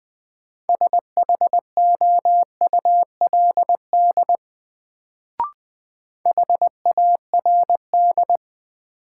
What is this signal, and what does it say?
Morse code, heard as a keyed tone, about 20 wpm, 700 Hz. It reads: SHOULD HARD
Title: Sentences from Top 300 Words - New Word Drill - 20wpm